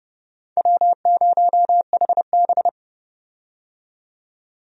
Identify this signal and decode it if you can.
Morse code, heard as a keyed tone, about 30 wpm, 700 Hz. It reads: W0HB
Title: USA Call Signs - 4 Characters - 30wpm